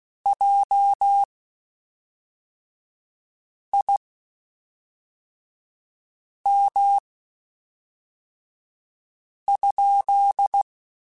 Morse code, heard as a keyed tone, about 16 wpm, spaced as for 3 wpm, 800 Hz. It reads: JIM?
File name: (spaced) / QSOs Spaced-mp3s / WD0BGP 1 @5 sp